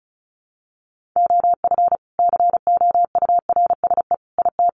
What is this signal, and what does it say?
Morse code, heard as a keyed tone, about 34 wpm, 700 Hz. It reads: OF COURSE IT
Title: Sentences from Top 500 Words 35wpm